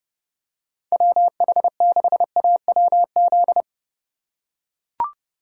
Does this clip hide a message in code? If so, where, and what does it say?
Yes: Morse code from 0.9 to 3.6 s, heard as a keyed tone, about 30 wpm, 700 Hz. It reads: WH6AWZ